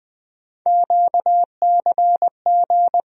Morse code, heard as a keyed tone, about 20 wpm, 700 Hz. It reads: QCG
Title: USA Call Sign Postfixes - 3 Letters 20wpm